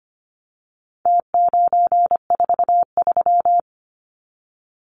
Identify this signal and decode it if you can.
Morse code, heard as a keyed tone, about 25 wpm, 700 Hz. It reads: T943
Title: Numbers - Contest Style - 4-Digit 25wpm